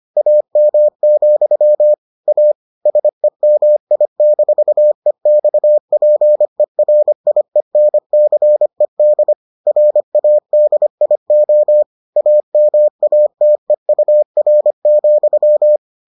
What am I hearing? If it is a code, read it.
Morse code, heard as a keyed tone, about 25 wpm, 600 Hz. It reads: AM, A SEMI-EXPERIENCED RADIO AMATEUR,